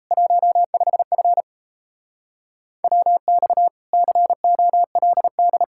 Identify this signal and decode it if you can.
Morse code, heard as a keyed tone, about 36 wpm, 700 Hz. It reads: 15F WX COLD